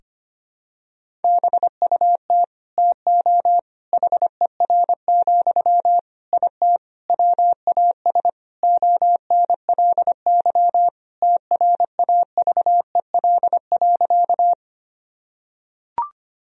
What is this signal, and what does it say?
Morse code, heard as a keyed tone, about 25 wpm, 700 Hz. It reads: BUT TO HER, IT WAS ONLY TRAVEL.
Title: Sentences from Top 1000 Words - Review Part 2 - 25wpm